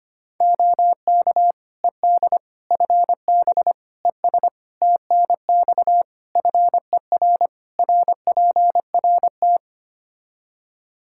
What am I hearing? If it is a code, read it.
Morse code, heard as a keyed tone, about 25 wpm, 700 Hz. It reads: OK ED FB ES TNX FER RPRT
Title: K1USN SST QSO Practice 25wpm